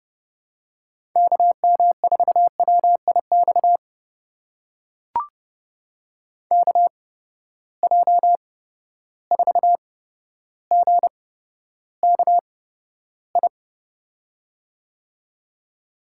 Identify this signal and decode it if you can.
Morse code, heard as a keyed tone, about 30 wpm, spaced as for 13 wpm, 700 Hz. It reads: KM4WIX KJ4GKI
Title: USA Call Signs - 6 Characters - 8x Character Spacing - 30wpm